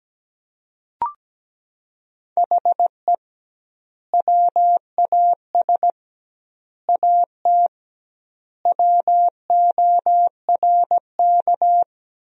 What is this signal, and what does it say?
Morse code, heard as a keyed tone, about 17 wpm, 700 Hz. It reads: HE WAS AT WORK
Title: 2x Word-Spacing - Sentences from Top 200 Words - 17wpm